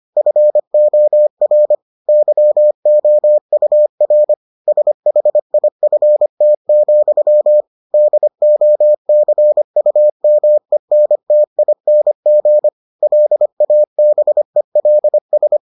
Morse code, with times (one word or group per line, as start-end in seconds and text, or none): 0.2-1.7 FOR
2.1-4.3 YOUR
4.7-7.6 SHIFT,
7.9-12.7 DOCUMENTING
13.0-15.6 LABELS